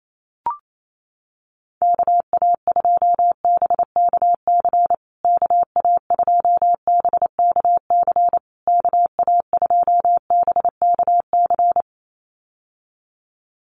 Morse code, heard as a keyed tone, about 28 wpm, 700 Hz. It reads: KA2BKC KA2BKC KA2BKC
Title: USA Call Signs Repeated 3x 28wpm